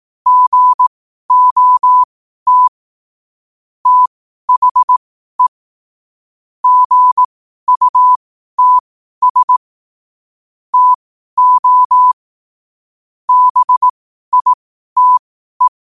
Morse code, heard as a keyed tone, about 18 wpm, spaced as for 12 wpm, 1000 Hz. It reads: GOT THE GUTS TO BITE
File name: qotdfiles / 12wpm